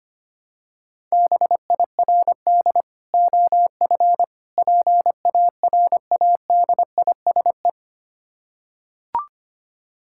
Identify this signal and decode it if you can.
Morse code, heard as a keyed tone, about 25 wpm, 700 Hz. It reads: BIRD OF PARADISE
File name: practice / 25wpm / Difficult Words with Hints - 25wpm